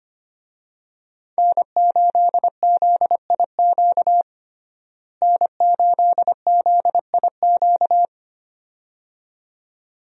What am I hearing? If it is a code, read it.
Morse code, heard as a keyed tone, about 25 wpm, 700 Hz. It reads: N8ZIQ N8ZIQ
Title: USA Call Signs - 5 Characters - Repeated 2x - 25wpm